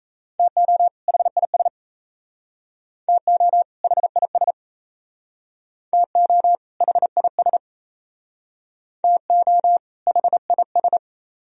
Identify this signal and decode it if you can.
Morse code, heard as a keyed tone, about 37 wpm, 700 Hz. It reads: TO HIS TO HIS TO HIS TO HIS